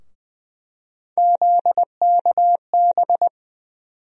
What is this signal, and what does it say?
Morse code, heard as a keyed tone, about 20 wpm, 700 Hz. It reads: ZKB
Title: USA Call Sign Postfixes - 3 Letters 20wpm